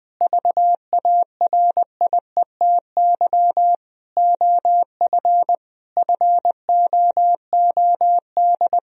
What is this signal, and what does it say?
Morse code, heard as a keyed tone, about 20 wpm, 700 Hz. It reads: VARIETY OF FOOD
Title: Sentences from Top 1000 Words - Review Part 4 - 20wpm